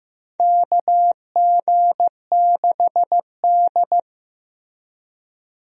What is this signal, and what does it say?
Morse code, heard as a keyed tone, about 16 wpm, 700 Hz. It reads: KG6D
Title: USA Call Signs - 4 Characters - Repeated 2x - 15wpm